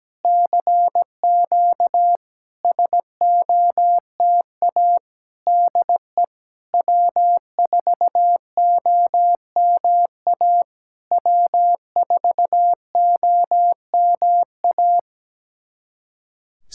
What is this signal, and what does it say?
Morse code, heard as a keyed tone, about 17 wpm, 700 Hz. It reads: CQ SOTA DE W4OMA W4OMA
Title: SOTA QSO Practice 17wpm